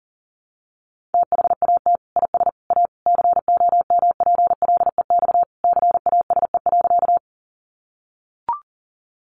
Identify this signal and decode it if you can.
Morse code, heard as a keyed tone, about 37 wpm, 700 Hz. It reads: THAT IS A COMPLEX CASE.
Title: Sentences from Top 900 Words 40wpm